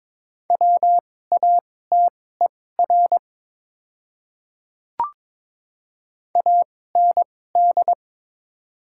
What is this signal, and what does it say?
Morse code, heard as a keyed tone, about 21 wpm, spaced as for 16 wpm, 700 Hz. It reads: WATER AND
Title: 2x Character Spacing - Top 200 Words - 22wpm